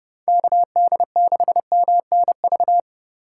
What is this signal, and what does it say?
Morse code, heard as a keyed tone, about 30 wpm, 700 Hz. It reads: KD6MNV
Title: USA Call Signs - 6 Characters - 2x Character Spacing - 30wpm